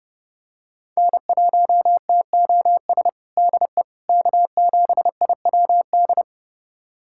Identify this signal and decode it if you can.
Morse code, heard as a keyed tone, about 30 wpm, 700 Hz. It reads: N1TOS DE K7IWD
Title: USA Callsign DE Callsign - 5 Characters - 30wpm